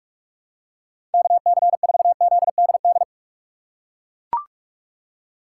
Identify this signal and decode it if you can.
Morse code, heard as a keyed tone, about 43 wpm, 700 Hz. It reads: KC4ZDD
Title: USA Call Signs - 6 Characters - 6x Character Spacing - 45wpm